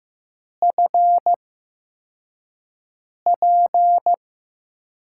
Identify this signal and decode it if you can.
Morse code, heard as a keyed tone, about 15 wpm, spaced as for 4 wpm, 700 Hz. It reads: FP
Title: USA Call Signs - 4 Characters - 8x Character Spacing - 15wpm